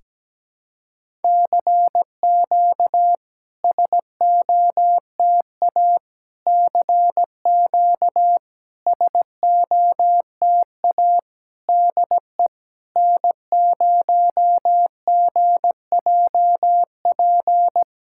Morse code, heard as a keyed tone, about 17 wpm, 690 Hz. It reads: CQ SOTA CQ SOTA DE N0GJP
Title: SOTA QSO Practice 17wpm